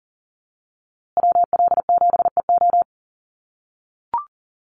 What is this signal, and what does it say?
Morse code, heard as a keyed tone, about 40 wpm, 700 Hz. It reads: WL7EO
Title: USA Call Signs Repeated 3x 40wpm